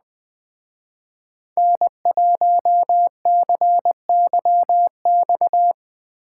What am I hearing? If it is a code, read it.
Morse code, heard as a keyed tone, about 20 wpm, 700 Hz. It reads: N1CYX
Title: Speed Racing - USA 1x3 Call Signs 17wpm